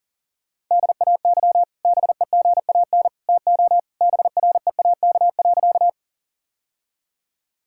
Morse code, heard as a keyed tone, about 39 wpm, 700 Hz. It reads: DAY BEGAN TO BREAK.
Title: Sentences from Top 1000 Words - Review Part 6 - 40wpm